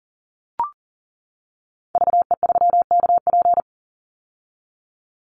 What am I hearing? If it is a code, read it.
Morse code, heard as a keyed tone, about 40 wpm, 700 Hz. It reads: VE3KP
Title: Call Signs - 40wpm